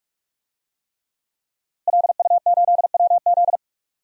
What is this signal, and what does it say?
Morse code, heard as a keyed tone, about 46 wpm, 700 Hz. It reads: LU8WZ
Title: DXCC-100 Call Signs - 5 Characters - Repeated 3x - 45wpm